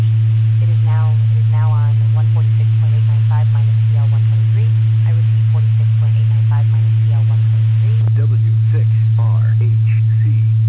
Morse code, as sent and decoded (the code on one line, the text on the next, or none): none